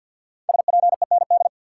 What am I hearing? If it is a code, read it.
Morse code, heard as a keyed tone, about 52 wpm, 700 Hz. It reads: SPEND
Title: Sentences from Top 800 Words - New Word Drill - 50wpm